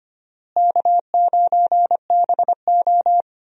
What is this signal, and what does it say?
Morse code, heard as a keyed tone, about 25 wpm, 700 Hz. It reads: K9BO
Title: USA Call Signs - 4 Characters - Repeated 2x - 25wpm